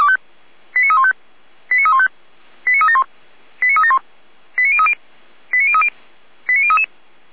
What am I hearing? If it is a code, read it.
Five-tone selective call (ZVEI), heard as quick runs of five tones, tones of about 70 ms. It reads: 89315 89315 89371 89371 89030 89030 89033
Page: feuerwehr-orlamuende